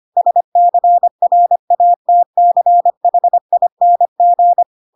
Morse code, heard as a keyed tone, about 25 wpm, 700 Hz. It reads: SCRATCHING
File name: Pinocchio / Pinocchio ch1 25 25 ik1hge